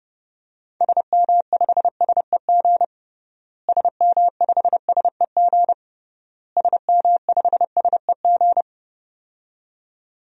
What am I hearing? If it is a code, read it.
Morse code, heard as a keyed tone, about 32 wpm, 700 Hz. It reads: SM5SEG SM5SEG SM5SEG
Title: DXCC-100 Call Signs - 6 Characters - Repeated 3x - 30wpm